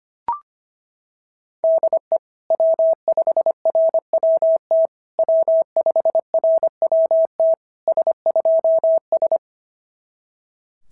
Morse code, heard as a keyed tone, about 24 wpm, 650 Hz. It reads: DE W5RWT W5RWT S2S
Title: SOTA QSO Practice 25wpm